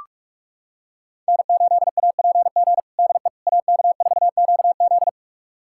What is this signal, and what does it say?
Morse code, heard as a keyed tone, about 47 wpm, 700 Hz. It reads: N8APG DE AK4QZ
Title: USA Callsign DE Callsign - 5 Characters - 45wpm